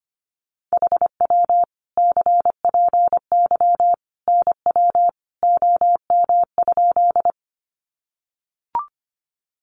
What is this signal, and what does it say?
Morse code, heard as a keyed tone, about 25 wpm, 700 Hz. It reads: HW CPY NW OM?
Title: Common QSO Phrases 25wpm